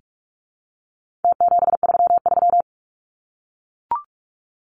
Morse code, heard as a keyed tone, about 44 wpm, 700 Hz. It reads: T733
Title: Numbers - Contest Style - 4-Digit 45wpm